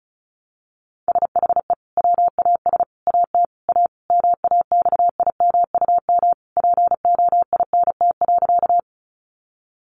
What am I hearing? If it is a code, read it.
Morse code, heard as a keyed tone, about 34 wpm, 700 Hz. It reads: SHE WAS AT A MAXIMUM POINT.
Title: Sentences from Top 1300 Words 35wpm